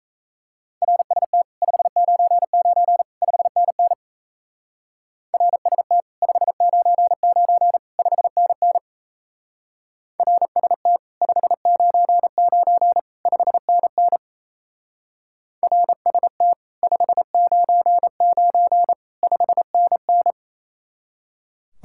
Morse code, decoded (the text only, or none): RST 599 5NN RST 599 5NN RST 599 5NN RST 599 5NN